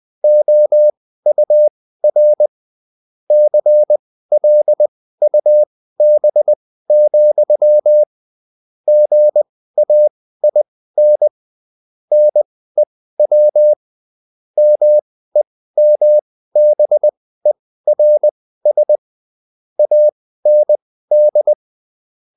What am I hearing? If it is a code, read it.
Morse code, heard as a keyed tone, about 20 wpm, spaced as for 14 wpm, 600 Hz. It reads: OUR CLUB, GAIN NEW MEMBERS AND